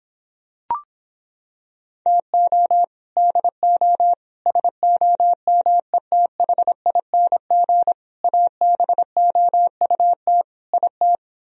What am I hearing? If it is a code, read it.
Morse code, heard as a keyed tone, about 26 wpm, 700 Hz. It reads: TO DO SOMETHING ABOUT IT